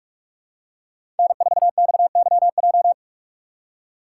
Morse code, heard as a keyed tone, about 46 wpm, 700 Hz. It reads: N4XYJ
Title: USA Call Signs - 5 Characters - 3x Character Spacing - 45wpm